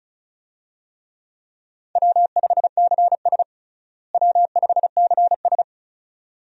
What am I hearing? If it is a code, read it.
Morse code, heard as a keyed tone, about 36 wpm, 700 Hz. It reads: W5CS W5CS
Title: USA Call Signs - 4 Characters - Repeated 2x - 35wpm